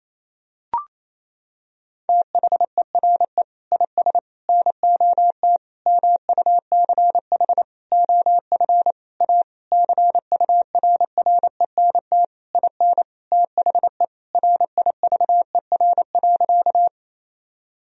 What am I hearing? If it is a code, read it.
Morse code, heard as a keyed tone, about 28 wpm, 700 Hz. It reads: THERE IS NOT MUCH OF A CURRENT IN THE RIVER.
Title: Sentences from Top 1000 Words - Review Part 4 - 28wpm